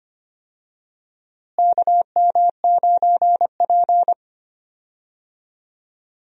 Morse code, heard as a keyed tone, about 25 wpm, 700 Hz. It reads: KM9P
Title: USA Call Signs - 4 Characters - Repeated 2x - 25wpm